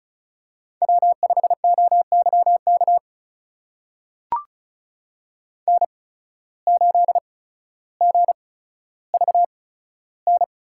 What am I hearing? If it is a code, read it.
Morse code, heard as a keyed tone, about 34 wpm, spaced as for 15 wpm, 700 Hz. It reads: W5OYK N8GVN
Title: USA Call Signs - 5 Characters - 8x Character Spacing - 35wpm